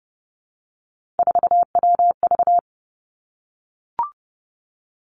Morse code, heard as a keyed tone, about 32 wpm, 700 Hz. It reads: 4WV
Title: Three Challenging Letter-Number Combinations 30wpm